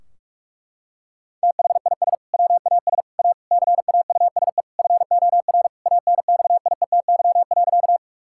Morse code, heard as a keyed tone, about 41 wpm, 700 Hz. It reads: THIS WAS A CAUSE FOR ANXIETY.
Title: Sentences from Top 1600 Words 45wpm